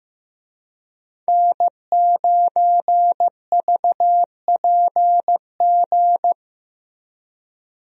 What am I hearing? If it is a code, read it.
Morse code, heard as a keyed tone, about 15 wpm, 700 Hz. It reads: N9VPG
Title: USA Call Signs - 5 Characters - Repeated 2x - 15wpm